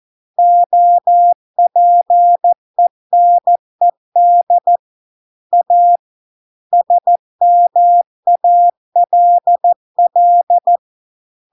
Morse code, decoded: OPENED A SMALL

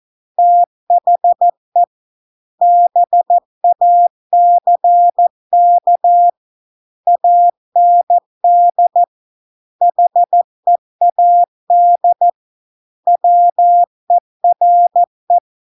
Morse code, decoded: THE BACK AND HEAD WERE

